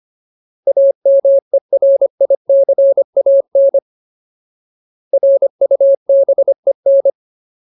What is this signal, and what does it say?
Morse code, heard as a keyed tone, about 25 wpm, 550 Hz. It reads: AMERICAN RUBEN
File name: Ultimate Word List / 25WPM Ultimate